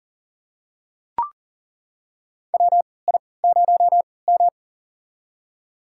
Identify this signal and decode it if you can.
Morse code, heard as a keyed tone, about 40 wpm, spaced as for 23 wpm, 700 Hz. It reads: WI0M
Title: USA Call Signs - 4 Characters - 3x Character Spacing - 40wpm